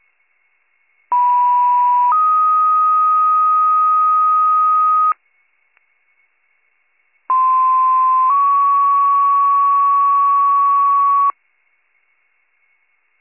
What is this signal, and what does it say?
Two-tone page, heard as a one-second tone, then a three-second tone, about 980 then 1250 Hz, about 1040 then 1120 Hz. It is heard from 1.1 to 11.3 s.